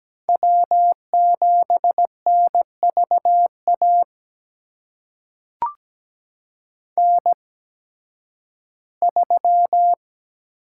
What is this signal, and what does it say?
Morse code, heard as a keyed tone, about 17 wpm, spaced as for 11 wpm, 700 Hz. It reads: W7NVA N3
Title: USA Call Signs - 8x Character Spacing - 17wpm